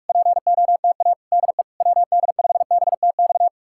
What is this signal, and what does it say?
Morse code, heard as a keyed tone, about 59 wpm, spaced as for 51 wpm, 700 Hz. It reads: POTA DE WD5BTX
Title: Speed-Racing Common QSO Phrases Encore - 45wpm